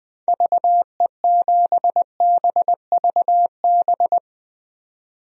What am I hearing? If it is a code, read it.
Morse code, heard as a keyed tone, about 21 wpm, 700 Hz. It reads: VE7BVB
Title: DXCC-100 Call Signs - 6 Characters - 20wpm